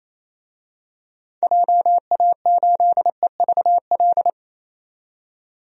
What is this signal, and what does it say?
Morse code, heard as a keyed tone, about 28 wpm, 700 Hz. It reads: JA8EVL
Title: DXCC-100 Call Signs - 6 Characters - 28wpm